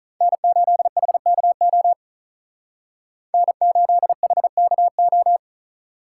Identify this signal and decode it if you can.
Morse code, heard as a keyed tone, about 43 wpm, 700 Hz. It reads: N8HKO N8HKO
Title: Speed Racing - USA 1x3 Call Signs 35wpm